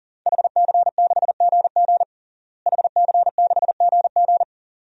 Morse code, heard as a keyed tone, about 37 wpm, 700 Hz. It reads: HC6GG HC6GG
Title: DXCC-100 Call Signs - 5 Characters - Repeated 2x - 40wpm